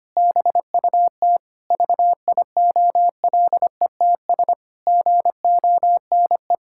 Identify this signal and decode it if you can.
Morse code, heard as a keyed tone, about 25 wpm, 700 Hz. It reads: BUT VIOLETS GONE
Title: Pangram Challenge v2 - 25wpm